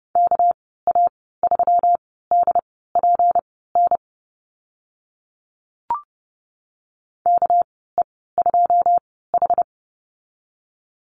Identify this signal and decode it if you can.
Morse code, heard as a keyed tone, about 30 wpm, spaced as for 17 wpm, 700 Hz. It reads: KA3DPN KE2H
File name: Lesson 37g - Letter x - Call Signs 30wpm